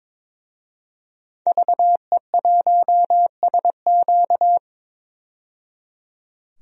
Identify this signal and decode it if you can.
Morse code, heard as a keyed tone, about 22 wpm, 700 Hz. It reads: VE1SQ
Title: DXCC-100 Call Signs - 5 Characters - 22wpm